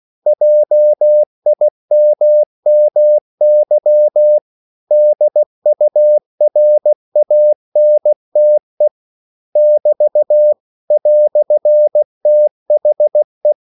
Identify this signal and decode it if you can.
Morse code, heard as a keyed tone, about 16 wpm, 600 Hz. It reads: JIMMY DURANTE <BT> "THE
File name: CW / Practice / EnglishQuotations0216